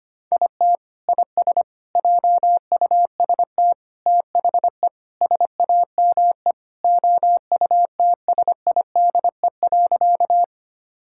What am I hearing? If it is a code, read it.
Morse code, heard as a keyed tone, about 25 wpm, 700 Hz. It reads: IT IS JUST THE SAME OUTSIDE.
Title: Sentences from Top 1000 Words - Review Part 6 - 25wpm